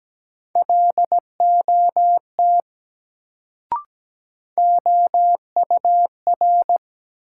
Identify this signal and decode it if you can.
Morse code, heard as a keyed tone, about 17 wpm, 700 Hz. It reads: LOT OUR
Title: Sets of 2 Words - New Word Drill - 17wpm